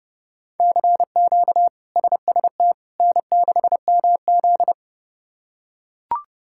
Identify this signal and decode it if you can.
Morse code, heard as a keyed tone, about 30 wpm, 700 Hz. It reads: CQ SST N6MZ
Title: K1USN SST QSO Practice - 30wpm